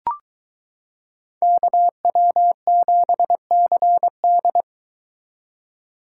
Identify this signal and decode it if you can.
Morse code, heard as a keyed tone, about 22 wpm, 700 Hz. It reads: KW7CD